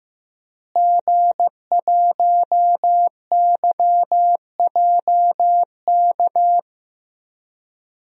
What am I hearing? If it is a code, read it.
Morse code, heard as a keyed tone, about 16 wpm, 700 Hz. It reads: G1YJK
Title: DXCC-100 Call Signs - 5 Characters - Repeated 2x - 15wpm